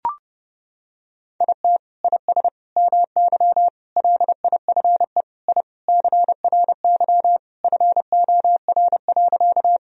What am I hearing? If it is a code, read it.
Morse code, heard as a keyed tone, about 30 wpm, 700 Hz. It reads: IT IS MY LIFE I CRY FOR.
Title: Sentences from Top 1000 Words - Review Part 2 - 30wpm